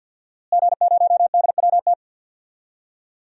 Morse code, heard as a keyed tone, about 47 wpm, 700 Hz. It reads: G0DWT
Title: DXCC-100 Call Signs - 5 Characters - Repeated 2x - 50wpm